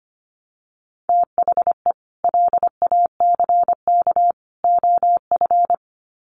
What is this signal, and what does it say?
Morse code, heard as a keyed tone, about 25 wpm, 700 Hz. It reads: THE LACK OF
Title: Sets of 3 Words 25wpm